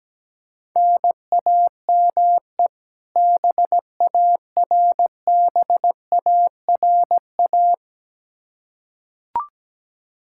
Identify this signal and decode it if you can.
Morse code, heard as a keyed tone, about 17 wpm, 700 Hz. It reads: NAME BARBARA
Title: Practice QSO - Exchange 1 - Beginning - 17wpm